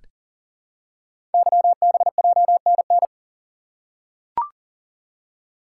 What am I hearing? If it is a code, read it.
Morse code, heard as a keyed tone, about 40 wpm, 700 Hz. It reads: YBJNN
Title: Code Groups - 5 Letters - 40wpm